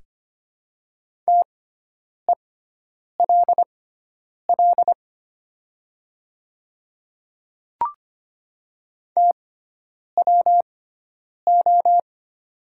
Morse code, heard as a keyed tone, about 25 wpm, spaced as for 9 wpm, 700 Hz. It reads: TELL TWO
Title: 6x Character Spacing - Top 200 Words - 25wpm